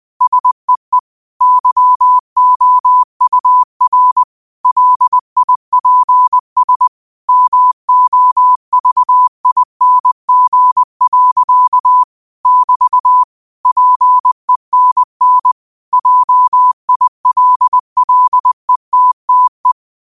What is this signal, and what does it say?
Morse code, heard as a keyed tone, about 20 wpm, 1000 Hz. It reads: SEE YOUR LIPS MOVING. <BT> PENN JILLETTE
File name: qotdfiles / 20wpm